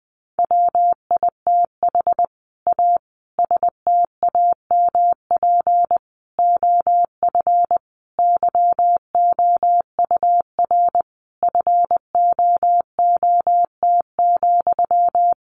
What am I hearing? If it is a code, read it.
Morse code, heard as a keyed tone, about 20 wpm, 700 Hz. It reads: WITH A STAMP OF YOUR FOOT,
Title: A Sound Of Thunder 20wpm